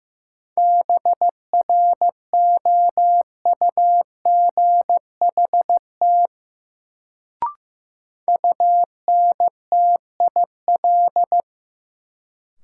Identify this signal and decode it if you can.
Morse code, heard as a keyed tone, about 15 wpm, 700 Hz. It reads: BROUGHT UNTIL
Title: Sentences from Top 500 Words - New Word Drill - 15wpm